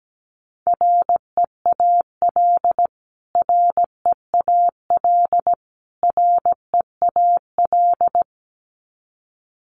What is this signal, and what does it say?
Morse code, heard as a keyed tone, about 17 wpm, 700 Hz. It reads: REAL REAL REAL
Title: Top 500 Words Repeated 17wpm